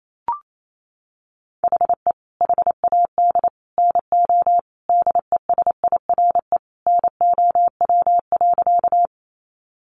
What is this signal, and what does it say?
Morse code, heard as a keyed tone, about 28 wpm, 700 Hz. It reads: HE HAD NO DESIRE NOW.
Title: Sentences from Top 1100 Words 28wpm